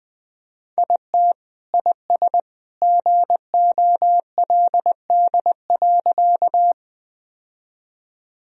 Morse code, heard as a keyed tone, about 21 wpm, 700 Hz. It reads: IT IS GOLD.
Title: Sentences from Top 1000 Words - Review Part 3 - 20wpm